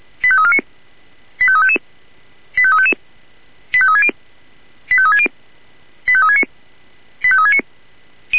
Five-tone selective call (ZVEI), heard as quick runs of five tones, tones of about 70 ms. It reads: 85349 85355 85360 85360 85369 85369